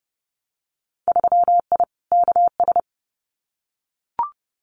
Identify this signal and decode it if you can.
Morse code, heard as a keyed tone, about 30 wpm, 700 Hz. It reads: 3I KS
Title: Winter Field Day Exchanges 30wpm